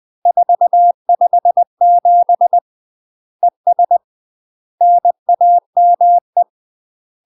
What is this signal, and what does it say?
Morse code, heard as a keyed tone, about 20 wpm, 700 Hz. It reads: 457 ES NAME
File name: UK MASTER QSOs TRAINING / 20WPM / UK MASTER QSOs 0065